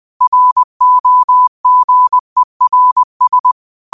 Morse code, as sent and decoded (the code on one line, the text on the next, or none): .-. --- --. .-. ...
ROGERS